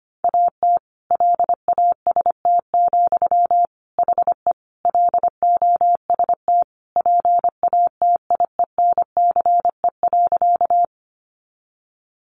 Morse code, heard as a keyed tone, about 25 wpm, 700 Hz. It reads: AT LAST, HE LOST PATIENCE.
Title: Sentences from Top 1400 Words 25wpm